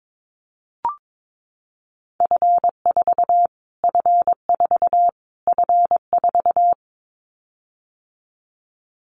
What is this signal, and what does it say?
Morse code, heard as a keyed tone, about 22 wpm, 700 Hz. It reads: F4 F4 F4
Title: Callsign Prefixes for Top 25 DXCC - Repeated 3x - 22wpm